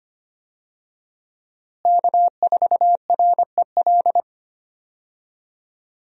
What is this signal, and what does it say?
Morse code, heard as a keyed tone, about 25 wpm, 700 Hz. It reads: K4REL
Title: USA Call Signs - 5 Characters - Repeated 3x - 25wpm